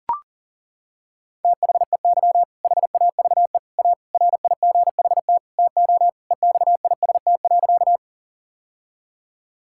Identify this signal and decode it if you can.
Morse code, heard as a keyed tone, about 40 wpm, 690 Hz. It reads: THEY HAVE A RIGHT TO EXIST.
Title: Sentences from Top 1000 Words - Review Part 5 - 40wpm